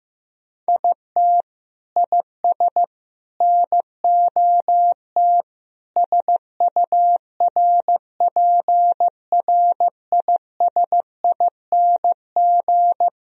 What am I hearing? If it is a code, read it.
Morse code, heard as a keyed tone, about 15 wpm, 700 Hz. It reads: IT IS NOT SURPRISING
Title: Speed-Racing - Sets of 5 Words 15wpm